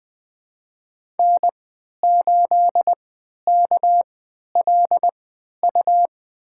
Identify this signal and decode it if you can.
Morse code, heard as a keyed tone, about 21 wpm, spaced as for 11 wpm, 700 Hz. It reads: N8KLU
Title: USA Call Signs - 5 Characters - 3x Character Spacing - 20wpm